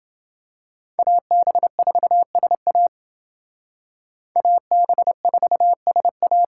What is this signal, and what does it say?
Morse code, heard as a keyed tone, about 29 wpm, 700 Hz. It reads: AB4SA AB4SA